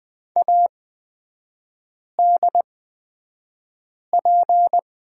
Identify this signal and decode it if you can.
Morse code, heard as a keyed tone, about 20 wpm, spaced as for 5 wpm, 700 Hz. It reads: ADP
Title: 229 Lesson 37e - Letter x - Three Characters 20wpm